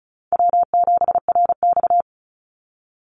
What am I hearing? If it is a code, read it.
Morse code, heard as a keyed tone, about 34 wpm, 700 Hz. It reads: W7RX
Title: Easy Call Signs Repeated 3x - 35wpm